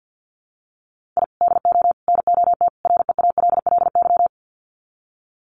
Morse code, heard as a keyed tone, about 50 wpm, 700 Hz. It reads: I DO NOT REALLY
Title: Sets of 4 Words 50wpm